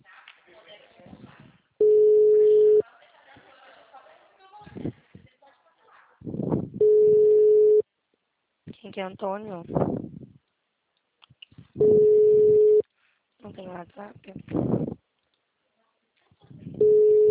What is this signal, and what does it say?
Ringback tone (Europe, 425 Hz), with speech.
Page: Sumare - SP